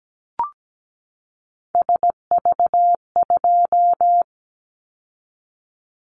Morse code, heard as a keyed tone, about 17 wpm, 700 Hz. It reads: SV2
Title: Callsign Prefixes for Top 50 DXCC - 17wpm